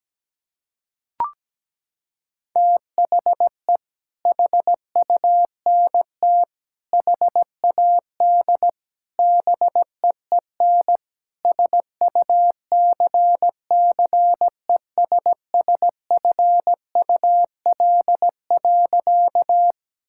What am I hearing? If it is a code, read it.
Morse code, heard as a keyed tone, about 17 wpm, 700 Hz. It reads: THE HUNT HAD BEEN SUCCESSFUL.
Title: Sentences from Top 1600 Words 17wpm